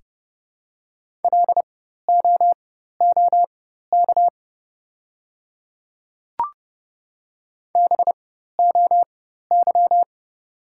Morse code, heard as a keyed tone, about 31 wpm, spaced as for 14 wpm, 700 Hz. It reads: LOOK BOY
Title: 4x Character Spacing - Top 200 Words - 30wpm